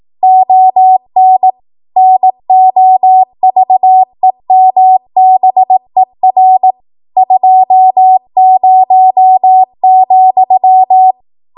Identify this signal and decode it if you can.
Morse code, heard as a keyed tone, about 18 wpm, 750 Hz. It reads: ON NOVEMBER 20,
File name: Archive / 18 WPM / 18WPM